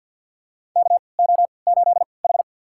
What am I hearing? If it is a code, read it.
Morse code, heard as a keyed tone, about 51 wpm, spaced as for 29 wpm, 700 Hz. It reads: KO8H
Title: USA Call Signs - 4 Characters - 3x Character Spacing - 50wpm